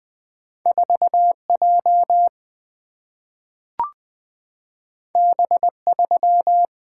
Morse code, heard as a keyed tone, about 20 wpm, 700 Hz. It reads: 4J B3
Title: Single-Letter Single-Number 20wpm